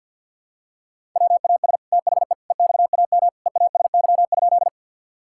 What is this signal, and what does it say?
Morse code, heard as a keyed tone, about 51 wpm, 700 Hz. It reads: WAS THE EXAM EASY?
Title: Sentences from Top 1000 Words - Review Part 5 - 50wpm